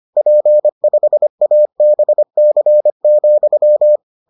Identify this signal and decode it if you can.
Morse code, heard as a keyed tone, about 25 wpm, 600 Hz. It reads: P5ABC,